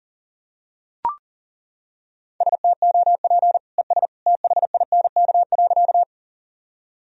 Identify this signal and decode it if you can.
Morse code, heard as a keyed tone, about 40 wpm, 700 Hz. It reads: STOP ES THINK.
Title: Sentences from Top 300 Words and Abbreviations 40wpm